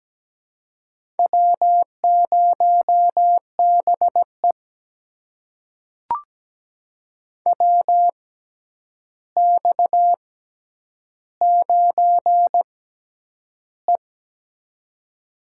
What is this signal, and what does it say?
Morse code, heard as a keyed tone, about 17 wpm, spaced as for 9 wpm, 700 Hz. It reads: W0BE WX9E
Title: USA Call Signs - 4 Characters - 6x Character Spacing - 17wpm